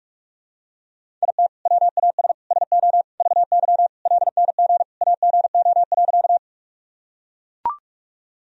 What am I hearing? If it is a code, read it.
Morse code, heard as a keyed tone, about 42 wpm, 700 Hz. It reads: IT WAS SO VY LNG AGO.
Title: Sentences from Top 500 Words and Abbreviations 45wpm